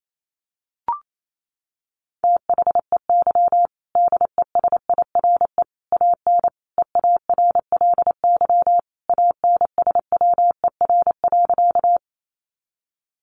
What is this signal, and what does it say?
Morse code, heard as a keyed tone, about 28 wpm, 700 Hz. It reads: THEY DESIRE AN EARLY ANSWER.
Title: Sentences from Top 1100 Words 28wpm